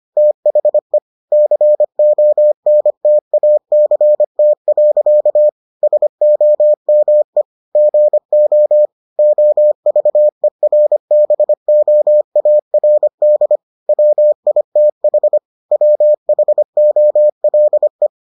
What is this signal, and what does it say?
Morse code, heard as a keyed tone, about 25 wpm, 600 Hz. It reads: THE CONTACT. SOME GO OVERBOARD WITH WHOLE